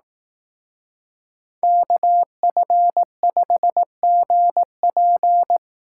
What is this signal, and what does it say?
Morse code, heard as a keyed tone, about 18 wpm, 700 Hz. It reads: KF5GP
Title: Speed Racing - USA Call Signs 15wpm